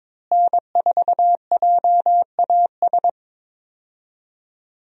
Morse code, heard as a keyed tone, about 22 wpm, 700 Hz. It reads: N4JAS
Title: USA Call Signs - 5 Characters - Repeated 3x - 22wpm